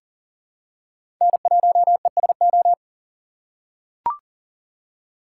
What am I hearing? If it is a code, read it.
Morse code, heard as a keyed tone, about 40 wpm, 700 Hz. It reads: N1ESO
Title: USA Call Signs - 5 Characters - 6x Character Spacing - 40wpm